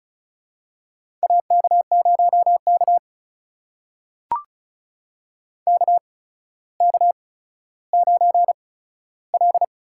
Morse code, heard as a keyed tone, about 34 wpm, spaced as for 15 wpm, 700 Hz. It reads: AK0K KK9L
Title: USA Call Signs - 4 Characters - 8x Character Spacing - 35wpm